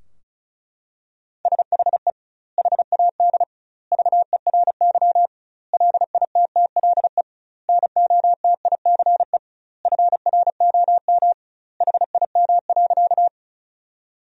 Morse code, heard as a keyed tone, about 36 wpm, 700 Hz. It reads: SHE HAD VERY LITTLE NOTICE FROM HIM.
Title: 2x Word-Spacing - Sentences from Top 400 Words - 35wpm